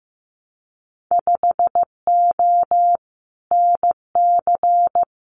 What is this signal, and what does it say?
Morse code, heard as a keyed tone, about 15 wpm, 700 Hz. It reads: 5O NC
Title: Winter Field Day Exchanges 15wpm